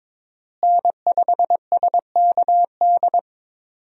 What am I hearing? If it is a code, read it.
Morse code, heard as a keyed tone, about 21 wpm, 700 Hz. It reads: N5SKD
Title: USA Call Signs - 5 Characters - Repeated 2x - 22wpm